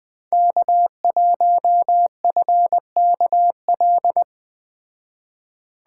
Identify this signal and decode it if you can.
Morse code, heard as a keyed tone, about 20 wpm, 700 Hz. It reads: K1FKL